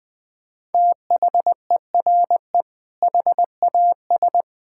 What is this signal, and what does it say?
Morse code, heard as a keyed tone, about 21 wpm, 700 Hz. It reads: THERE HAS
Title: Sets of 8 Words - 20wpm